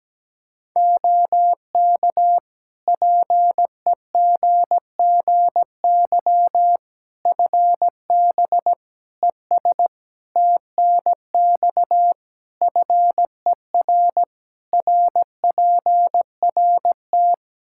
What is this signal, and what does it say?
Morse code, heard as a keyed tone, about 17 wpm, 700 Hz. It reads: OK PEGGY FB ES TNX FER RPRT
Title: Practice QSO - Exchange 2 - Info - 17wpm